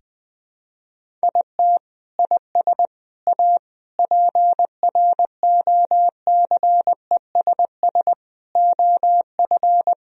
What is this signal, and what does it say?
Morse code, heard as a keyed tone, about 21 wpm, 700 Hz. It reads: IT IS A PROCESS OF
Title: Sentences from Top 1000 Words - Review Part 4 - 20wpm